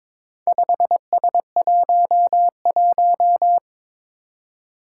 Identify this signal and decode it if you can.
Morse code, heard as a keyed tone, about 23 wpm, 700 Hz. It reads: 5S11
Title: Four Challenging Letter-Number Combinations 22wpm